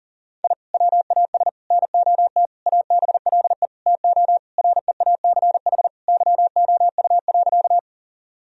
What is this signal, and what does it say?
Morse code, heard as a keyed tone, about 40 wpm, 700 Hz. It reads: I WAS NOT ABLE TO REACH YOU.
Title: Sentences from Top 1000 Words - Review Part 2 - 40wpm